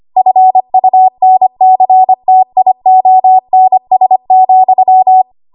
Morse code, heard as a keyed tone, about 25 wpm, 750 Hz. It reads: FUNCTIONS,
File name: Archive / 25 WPM / 25WPM